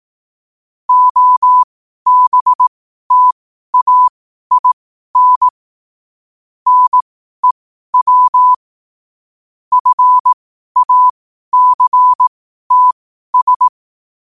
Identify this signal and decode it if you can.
Morse code, heard as a keyed tone, about 18 wpm, spaced as for 12 wpm, 1000 Hz. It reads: OBTAIN NEW FACTS
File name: qotdfiles / 12wpm